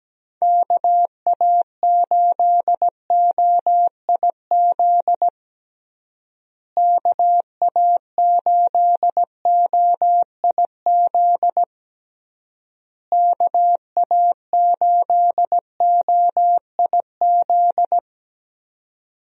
Morse code, heard as a keyed tone, about 17 wpm, 700 Hz. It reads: KA8OIZ KA8OIZ KA8OIZ
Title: USA Call Signs - 6 Characters - Repeated 3x - 17wpm